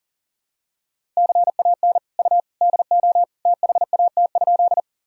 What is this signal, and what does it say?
Morse code, heard as a keyed tone, about 40 wpm, 700 Hz. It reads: CAN U DO THAT?
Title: Sentences from Top 100 Words and Abbreviations 40wpm